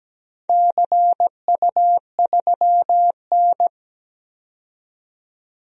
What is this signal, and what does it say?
Morse code, heard as a keyed tone, about 17 wpm, 700 Hz. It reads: CU3N
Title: DXCC-100 Call Signs - 4 Characters - 17wpm